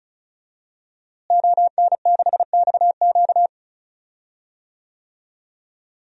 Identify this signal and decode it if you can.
Morse code, heard as a keyed tone, about 36 wpm, 700 Hz. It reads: ON6XQ
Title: DXCC-100 Call Signs - 5 Characters - Repeated 2x - 35wpm